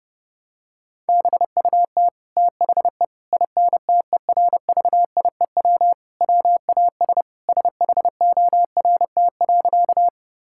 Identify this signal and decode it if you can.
Morse code, heard as a keyed tone, about 30 wpm, 700 Hz. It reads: BUT THE INTERVIEW WAS SHORT.
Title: Sentences from Top 1600 Words 30wpm